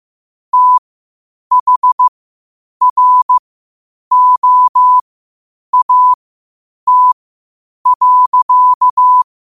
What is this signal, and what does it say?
Morse code, heard as a keyed tone, about 15 wpm, spaced as for 8 wpm, 1000 Hz. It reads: THROAT.
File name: qotdfiles / 8wpm